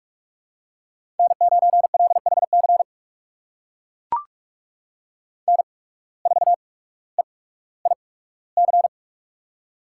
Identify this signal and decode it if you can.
Morse code, heard as a keyed tone, about 44 wpm, spaced as for 19 wpm, 700 Hz. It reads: N9LHC N4EIC
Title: USA Call Signs - 5 Characters - 8x Character Spacing - 45wpm